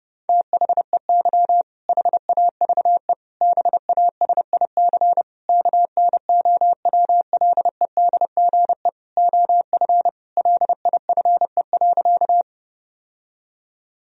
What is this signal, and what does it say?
Morse code, heard as a keyed tone, about 30 wpm, 700 Hz. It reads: THEY HAVE BASIC KNOWLEDGE OF LIFE.
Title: Sentences from Top 1000 Words - Review Part 3 - 30wpm